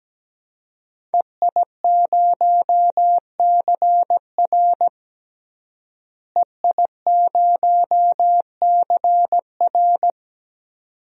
Morse code, heard as a keyed tone, about 17 wpm, 700 Hz. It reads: EI0CR EI0CR
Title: DXCC-100 Call Signs - 5 Characters - Repeated 2x - 17wpm